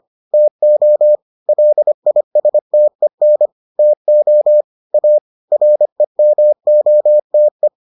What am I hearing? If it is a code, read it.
Morse code, heard as a keyed tone, about 25 wpm, 600 Hz. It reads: TO LISTEN TO A REMOTE